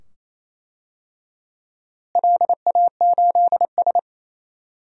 Morse code, heard as a keyed tone, about 28 wpm, 700 Hz. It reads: LA8S
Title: DXCC-100 Call Signs - 4 Characters - Repeated 2x - 28wpm